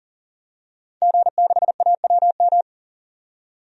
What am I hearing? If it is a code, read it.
Morse code, heard as a keyed tone, about 38 wpm, 700 Hz. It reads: G6AWM
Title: DXCC-100 Call Signs - 5 Characters - 40wpm